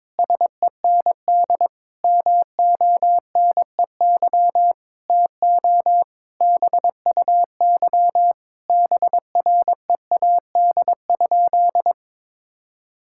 Morse code, heard as a keyed tone, about 22 wpm, 700 Hz. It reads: SEND MONEY TO BUY BREAD?
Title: Sentences from Top 1600 Words 22wpm